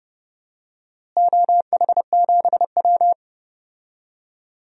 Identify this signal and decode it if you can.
Morse code, heard as a keyed tone, about 30 wpm, 700 Hz. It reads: OH7W
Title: DXCC-100 Call Signs - 4 Characters - Repeated 2x - 30wpm